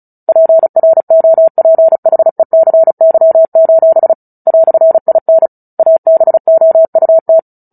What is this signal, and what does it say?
Morse code, heard as a keyed tone, about 34 wpm, 650 Hz. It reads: PROPHECY: "IN ABOUT
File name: Book Two / 09-Chapter 9 - 35 WPM